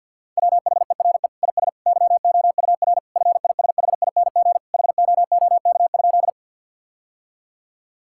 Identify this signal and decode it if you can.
Morse code, heard as a keyed tone, about 50 wpm, 700 Hz. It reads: WHERE IS YOUR FISHING HOOK?
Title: Sentences from Top 1600 Words 50wpm